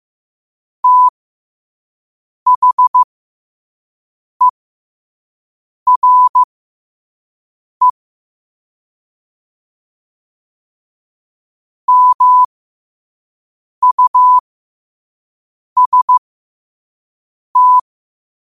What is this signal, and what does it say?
Morse code, heard as a keyed tone, about 15 wpm, spaced as for 5 wpm, 1000 Hz. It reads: THERE MUST
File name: qotdfiles / 5wpm